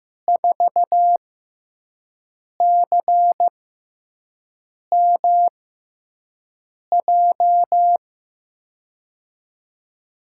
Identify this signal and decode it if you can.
Morse code, heard as a keyed tone, about 15 wpm, spaced as for 5 wpm, 700 Hz. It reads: 4CMJ
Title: USA Call Signs - 6 Characters - 6x Character Spacing - 15wpm